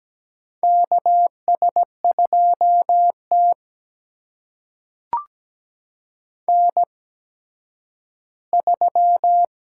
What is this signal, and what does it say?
Morse code, heard as a keyed tone, about 17 wpm, spaced as for 10 wpm, 700 Hz. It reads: KS2T N3
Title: USA Call Signs - 8x Character Spacing - 17wpm